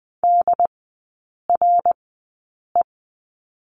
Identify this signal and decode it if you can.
Morse code, heard as a keyed tone, about 20 wpm, spaced as for 8 wpm, 700 Hz. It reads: DRE
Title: 198 Lesson 32e - Letter k - Three Characters 20wpm